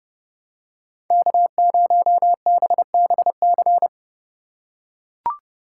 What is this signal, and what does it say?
Morse code, heard as a keyed tone, about 30 wpm, 700 Hz. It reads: K0BBC
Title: USA Call Signs - 2x Character Spacing - 30wpm